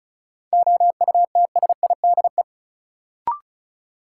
Morse code, heard as a keyed tone, about 36 wpm, 700 Hz. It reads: OUTSIDE
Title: Sentences from Top 900 Words - New Word Drill - 35wpm